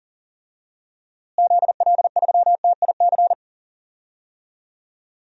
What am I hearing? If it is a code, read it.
Morse code, heard as a keyed tone, about 40 wpm, 700 Hz. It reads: ZL3TIC
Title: DXCC-100 Call Signs - 6 Characters - Repeated 2x - 40wpm